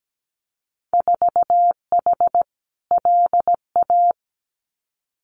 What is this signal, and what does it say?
Morse code, heard as a keyed tone, about 17 wpm, 700 Hz. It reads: 4H LA
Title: Winter Field Day Exchanges Encore 17wpm